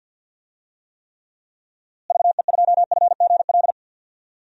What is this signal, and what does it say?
Morse code, heard as a keyed tone, about 46 wpm, 700 Hz. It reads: VE2FGL
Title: DXCC-100 Call Signs - 6 Characters - Repeated 3x - 50wpm